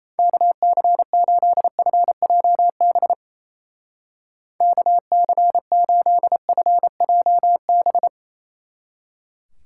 Morse code, heard as a keyed tone, about 33 wpm, 700 Hz. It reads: KC8FJB KC8FJB
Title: Speed Racing - USA 2x3 Call Signs 28wpm